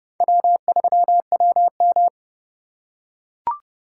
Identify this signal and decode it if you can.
Morse code, heard as a keyed tone, about 28 wpm, 700 Hz. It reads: W3WM